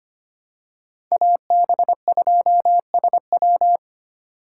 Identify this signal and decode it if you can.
Morse code, heard as a keyed tone, about 25 wpm, 700 Hz. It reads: AB2SW
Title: USA Call Signs - 5 Characters - 4x Character Spacing - 25wpm